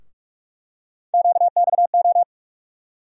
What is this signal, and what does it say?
Morse code, heard as a keyed tone, about 47 wpm, 700 Hz. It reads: QXO
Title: USA Call Sign Postfixes - 3 Letters 45wpm